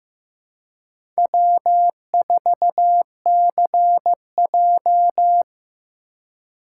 Morse code, heard as a keyed tone, about 15 wpm, 700 Hz. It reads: W4CJ
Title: USA Call Signs - 3x Character Spacing - 15wpm